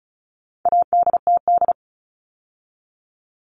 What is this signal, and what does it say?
Morse code, heard as a keyed tone, about 34 wpm, 700 Hz. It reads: ADTD
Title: All Cut Numbers - 4-Digit 35wpm